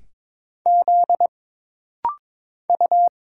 Morse code, heard as a keyed tone, about 22 wpm, 700 Hz. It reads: Z U
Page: This rapid-fire timing is geared to push your ICR (Instant Character Recognition) skills.